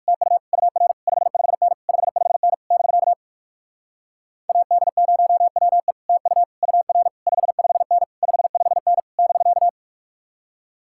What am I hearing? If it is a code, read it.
Morse code, heard as a keyed tone, about 56 wpm, 700 Hz. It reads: TU UR 55N 55N <BK> AD0WE TU UR 55N 55N <BK>
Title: Speed-Racing Common QSO Phrases - 45wpm